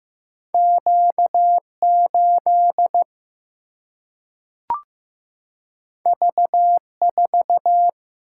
Morse code, heard as a keyed tone, about 15 wpm, 700 Hz. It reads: Q 8 V 4
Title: Single-Letter Single-Number 15wpm